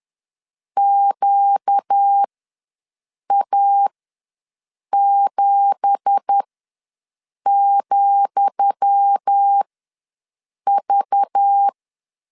Morse code, heard as a keyed tone, about 10 wpm, spaced as for 6 wpm, 800 Hz. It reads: QA7,V
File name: LEZIONI / 10 SIGNS / 10D COMA ws 10wpm 4314